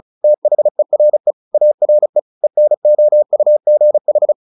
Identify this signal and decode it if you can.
Morse code, heard as a keyed tone, about 35 wpm, 600 Hz. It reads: THERE ARE ENOUGH
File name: the old man and the sea 2 35wpm